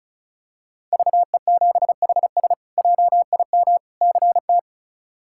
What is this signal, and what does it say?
Morse code, heard as a keyed tone, about 34 wpm, 700 Hz. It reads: VE7HS JIM CT
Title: K1USN SST QSO Practice - 35wpm